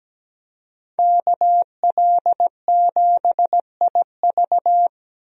Morse code, heard as a keyed tone, about 17 wpm, 700 Hz. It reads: KL7IV
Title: DXCC-100 Call Signs - 5 Characters - 17wpm